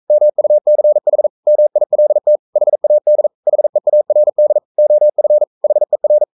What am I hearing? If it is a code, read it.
Morse code, heard as a keyed tone, about 42 wpm, 600 Hz. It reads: MUCH MILT HAD HEARD OF HER
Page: Romanen Never af Ken Follet på engelsk.